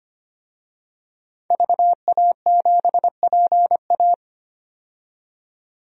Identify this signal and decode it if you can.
Morse code, heard as a keyed tone, about 25 wpm, 700 Hz. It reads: VA7PA
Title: DXCC-100 Call Signs - 5 Characters - 25wpm